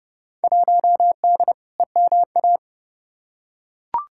Morse code, heard as a keyed tone, about 30 wpm, 700 Hz. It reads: 1D EMA
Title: Field Day Exchanges - 30wpm